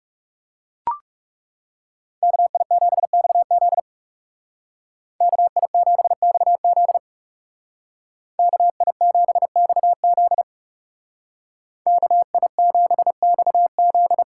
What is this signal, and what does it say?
Morse code, heard as a keyed tone, about 39 wpm, 700 Hz. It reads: KI7XZ KI7XZ KI7XZ KI7XZ